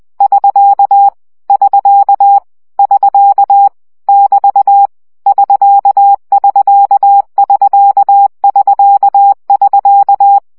Morse code, heard as a keyed tone, about 20 wpm, 800 Hz. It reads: <SK> <SK> <SK> <BT> <SK><SK><SK><SK><SK>
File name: CW gyakorlas sk 17 baud